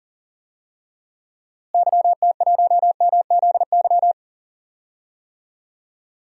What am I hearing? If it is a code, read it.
Morse code, heard as a keyed tone, about 40 wpm, 700 Hz. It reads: YT1MZY
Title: DXCC-100 Call Signs - 6 Characters - Repeated 2x - 40wpm